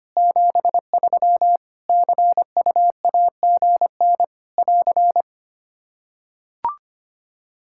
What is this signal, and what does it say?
Morse code, heard as a keyed tone, about 25 wpm, 700 Hz. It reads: 73 CUAGN <AR>
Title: K1USN SST QSO Practice 25wpm